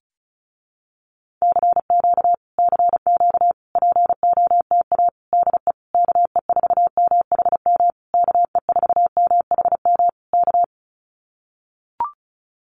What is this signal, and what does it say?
Morse code, heard as a keyed tone, about 34 wpm, 700 Hz. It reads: CQ CQ POTA DE KE4MHM KE4MHM K
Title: Common QSO Phrases Encore 35wpm